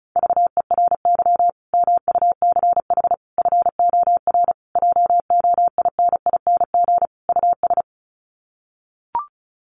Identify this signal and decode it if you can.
Morse code, heard as a keyed tone, about 34 wpm, 700 Hz. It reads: VERY MUCH FOR JOINING US
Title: Sets of 5 Words 35wpm